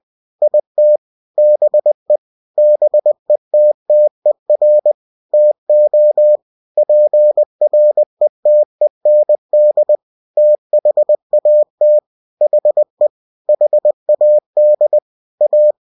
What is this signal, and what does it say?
Morse code, heard as a keyed tone, about 20 wpm, 600 Hz. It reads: IT BE BETTER TO PRETEND THAT HE HAD A
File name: cw-books / winnie-the-pooh / winnie-20wpm-page-0041